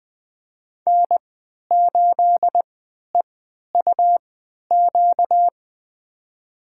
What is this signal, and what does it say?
Morse code, heard as a keyed tone, about 20 wpm, spaced as for 11 wpm, 700 Hz. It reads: N8EUQ
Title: USA Call Signs - 5 Characters - 3x Character Spacing - 20wpm